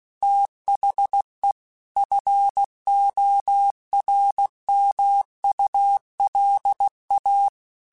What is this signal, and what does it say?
Morse code, heard as a keyed tone, about 16 wpm, 800 Hz. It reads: THE FORMULA
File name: Technical Subjects / Technical Subjects-mp3s / XL @15